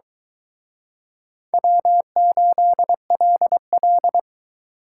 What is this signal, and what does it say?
Morse code, heard as a keyed tone, about 24 wpm, 700 Hz. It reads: W8LL